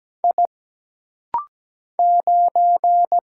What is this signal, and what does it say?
Morse code, heard as a keyed tone, about 17 wpm, 700 Hz. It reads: I 9
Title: Single Letter-Number Alt - Rapid-Fire 17wpm